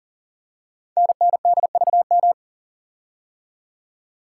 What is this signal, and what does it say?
Morse code, heard as a keyed tone, about 40 wpm, 690 Hz. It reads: NNDVM
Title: Code Groups - 5 Letters - 40wpm